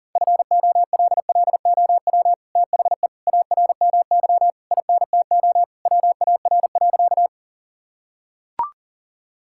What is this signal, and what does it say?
Morse code, heard as a keyed tone, about 37 wpm, 700 Hz. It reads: FOLLOW THE ARMY INTO WAR.
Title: Sentences from Top 1000 Words - Review Part 5 - 40wpm